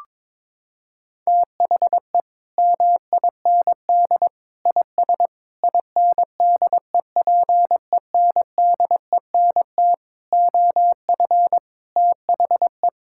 Morse code, heard as a keyed tone, about 22 wpm, 700 Hz. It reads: THE MIND IS INDEPENDENT OF THE
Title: Sentences from Top 1000 Words - Review Part 6 - 22wpm